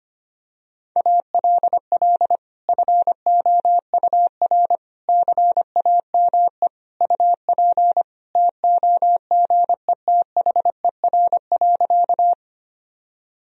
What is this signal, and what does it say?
Morse code, heard as a keyed tone, about 25 wpm, 700 Hz. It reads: ALL FOUR CAME UP TOGETHER.
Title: Sentences from Top 1000 Words - Review Part 1 - 25wpm